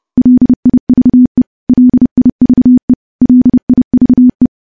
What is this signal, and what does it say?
Morse code, heard as a keyed tone, about 31 wpm, 250 Hz. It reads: LIVE LIVE LIVE
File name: morse / morse-training-30WPM